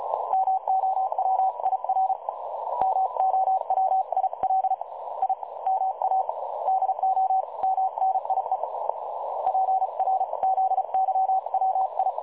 Morse code, heard as a keyed tone, about 33 wpm, 760 Hz, with noise. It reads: MOWIR MOWIL I MA DO MNIE PRZYWI
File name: dlazaawansowanych / nagraniazpasma / websdr recording 3565.0kHz